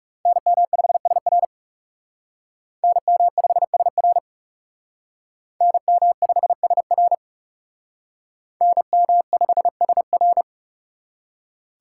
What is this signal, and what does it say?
Morse code, heard as a keyed tone, about 40 wpm, 700 Hz. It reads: NM5SR NM5SR NM5SR NM5SR